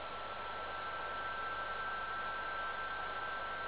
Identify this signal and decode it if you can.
PSK31, about 31.25 baud.